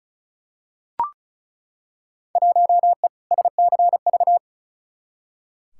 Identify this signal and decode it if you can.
Morse code, heard as a keyed tone, about 34 wpm, 700 Hz. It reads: 1E SCV
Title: Field Day Exchanges - 35wpm